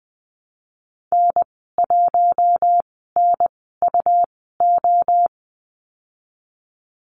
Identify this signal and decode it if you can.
Morse code, heard as a keyed tone, about 19 wpm, spaced as for 14 wpm, 700 Hz. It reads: N1NUO
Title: 122 Lesson 20g - Number 3 - Call Signs 20wpm